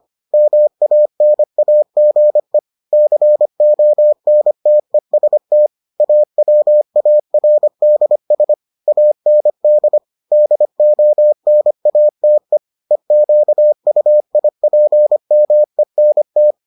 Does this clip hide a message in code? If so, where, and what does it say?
Yes: Morse code from 0.3 to 16.5 s, heard as a keyed tone, about 25 wpm, 600 Hz. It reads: MANAGE CONTEST AWARDS AND DONATE EQUIPMENT